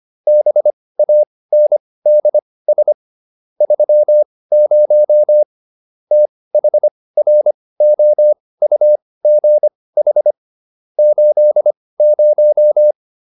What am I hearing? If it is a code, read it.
Morse code, heard as a keyed tone, about 25 wpm, spaced as for 18 wpm, 600 Hz. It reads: BANDS 30 THROUGH 80
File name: SS211 18